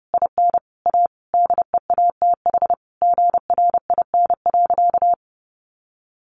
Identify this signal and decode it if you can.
Morse code, heard as a keyed tone, about 30 wpm, 700 Hz. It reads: IN A DEATH GRIN.
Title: A Sound Of Thunder 30wpm